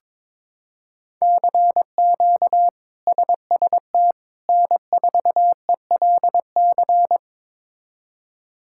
Morse code, heard as a keyed tone, about 22 wpm, 700 Hz. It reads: CQ SST N4ELC
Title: K1USN SST QSO Practice - 22wpm